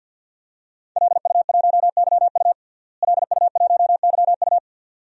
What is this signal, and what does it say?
Morse code, heard as a keyed tone, about 52 wpm, 700 Hz. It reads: LU1YU LU1YU
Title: DXCC-100 Call Signs - 5 Characters - Repeated 2x - 50wpm